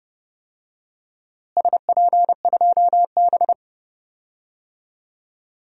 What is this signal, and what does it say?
Morse code, heard as a keyed tone, about 30 wpm, 700 Hz. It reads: SP2B
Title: DXCC-100 Call Signs - 4 Characters - Repeated 2x - 30wpm